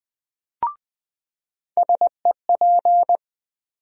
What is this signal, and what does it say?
Morse code, heard as a keyed tone, about 20 wpm, 700 Hz. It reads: SEP
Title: USA Call Sign Postfixes - 3 Letters 20wpm